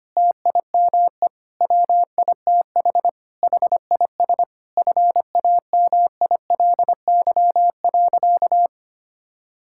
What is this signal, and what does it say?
Morse code, heard as a keyed tone, about 25 wpm, 700 Hz. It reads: TIME WITH HIS FAMILY.
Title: Sentences from Top 1000 Words - Review Part 6 - 25wpm